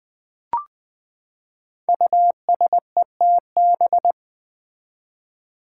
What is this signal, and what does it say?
Morse code, heard as a keyed tone, about 20 wpm, 700 Hz. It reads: USETB
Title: Five Challenging Letter-Number Combinations 20wpm